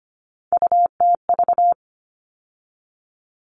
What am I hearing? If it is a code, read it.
Morse code, heard as a keyed tone, about 25 wpm, 700 Hz. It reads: UTV
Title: All Cut Numbers - 3-Digit 25wpm